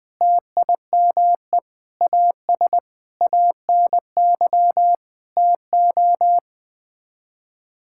Morse code, heard as a keyed tone, about 21 wpm, 700 Hz. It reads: TIME AS ANY TO
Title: Sets of 8 Words - 20wpm